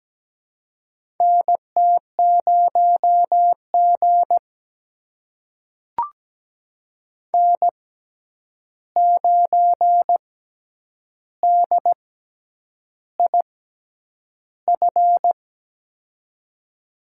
Morse code, heard as a keyed tone, about 17 wpm, spaced as for 8 wpm, 700 Hz. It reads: NT0G N9DIF
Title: USA Call Signs - 6x Character Spacing - 17wpm